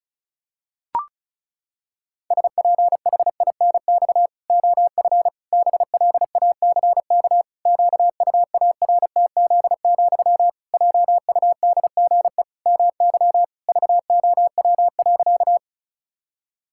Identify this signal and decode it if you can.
Morse code, heard as a keyed tone, about 34 wpm, 700 Hz. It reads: SPHINX OF BLACK QUARTZ, JUDGE MY VOW.
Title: Pangram Challenge v2 - 35wpm